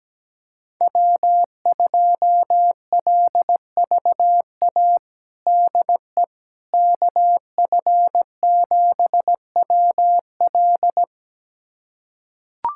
Morse code, heard as a keyed tone, about 17 wpm, 700 Hz. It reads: W2LVA DE KF7WL
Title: USA Callsign DE Callsign - 5 Characters - 17wpm